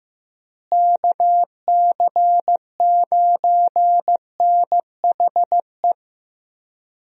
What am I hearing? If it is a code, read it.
Morse code, heard as a keyed tone, about 15 wpm, 700 Hz. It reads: KC9NHE
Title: USA Call Signs - 6 Characters - Repeated 3x - 15wpm